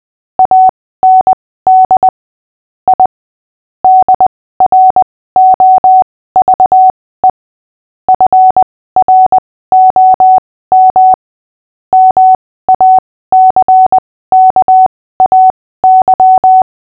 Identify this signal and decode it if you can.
Morse code, heard as a keyed tone, about 20 wpm, spaced as for 15 wpm, 750 Hz. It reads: AND I DROVE FROM MACKAY
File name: cw / fists / Short-Story-5-15wpm